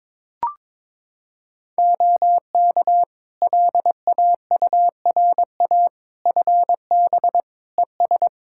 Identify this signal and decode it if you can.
Morse code, heard as a keyed tone, about 22 wpm, 700 Hz. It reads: OK LAURA FB ES
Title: K1USN SST QSO Practice 22wpm